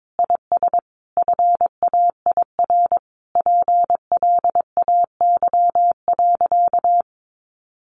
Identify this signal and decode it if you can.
Morse code, heard as a keyed tone, about 22 wpm, 700 Hz. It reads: IS FAIR PLAY.
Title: Sentences from Top 1400 Words 22wpm